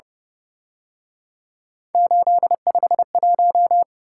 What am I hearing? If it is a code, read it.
Morse code, heard as a keyed tone, about 31 wpm, 700 Hz. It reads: 851
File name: Speed-Racing - Three Letter-Number Combinations - 25wpm